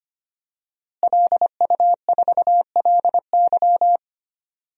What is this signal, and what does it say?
Morse code, heard as a keyed tone, about 25 wpm, 700 Hz. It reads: LU4LY
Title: DXCC-100 Call Signs - 5 Characters - Repeated 3x - 25wpm